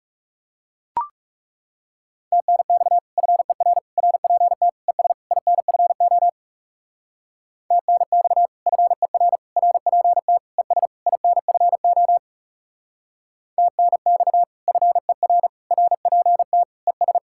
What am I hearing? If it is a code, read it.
Morse code, heard as a keyed tone, about 36 wpm, 700 Hz. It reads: TNX FER RPT ES INFO TNX FER RPT ES INFO TNX FER RPT ES
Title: Speed-Racing Common QSO Phrases - 30wpm